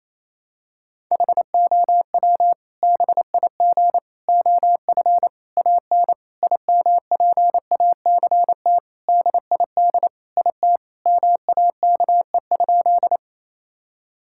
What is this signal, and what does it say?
Morse code, heard as a keyed tone, about 28 wpm, 700 Hz. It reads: HOW BIG OF AN IMPACT DID IT MAKE?
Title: Sentences from Top 1000 Words - Review Part 6 - 28wpm